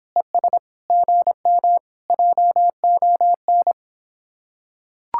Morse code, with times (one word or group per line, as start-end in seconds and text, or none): 0.2-0.6 ES
0.9-1.8 GM
2.1-3.7 JON